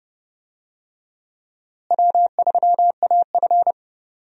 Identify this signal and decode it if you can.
Morse code, heard as a keyed tone, about 30 wpm, 700 Hz. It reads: W3AF
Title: USA Call Signs - 4 Characters - Repeated 2x - 30wpm